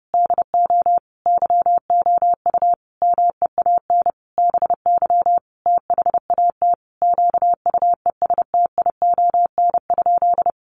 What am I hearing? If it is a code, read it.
Morse code, heard as a keyed tone, about 30 wpm, 700 Hz. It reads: DO YOU MEAN BY THAT QUESTION?
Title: Sentences from Top 400 Words 30wpm